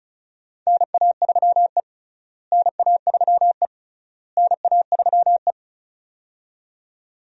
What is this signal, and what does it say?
Morse code, heard as a keyed tone, about 34 wpm, 700 Hz. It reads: NA3E NA3E NA3E
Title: USA Call Signs - 4 Characters - Repeated 3x - 35wpm